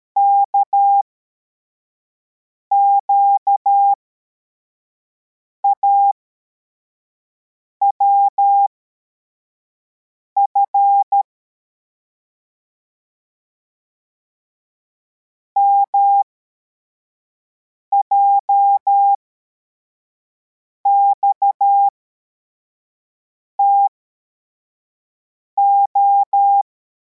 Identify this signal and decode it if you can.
Morse code, heard as a keyed tone, about 13 wpm, spaced as for 4 wpm, 800 Hz. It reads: KQAWF MJXTO
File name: formation / cw / Text81